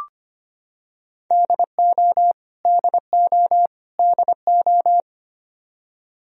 Morse code, heard as a keyed tone, about 25 wpm, 700 Hz. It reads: DO DO DO
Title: Top 100 Words Repeated 25wpm